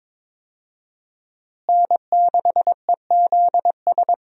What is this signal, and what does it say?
Morse code, heard as a keyed tone, about 22 wpm, 700 Hz. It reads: N6EZS
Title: USA Call Signs - 5 Characters - Repeated 2x - 22wpm